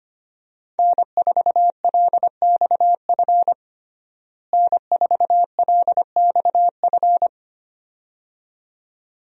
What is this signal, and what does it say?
Morse code, heard as a keyed tone, about 25 wpm, 700 Hz. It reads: N4LXF N4LXF
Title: USA Call Signs - 5 Characters - Repeated 2x - 25wpm